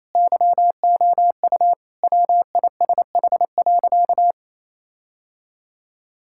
Morse code, heard as a keyed tone, about 28 wpm, 700 Hz. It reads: YOU WISH.
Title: Sentences from Top 1600 Words 28wpm